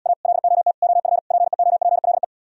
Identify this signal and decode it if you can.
Morse code, heard as a keyed tone, about 77 wpm, 700 Hz. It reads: A LOT OF PEOPLE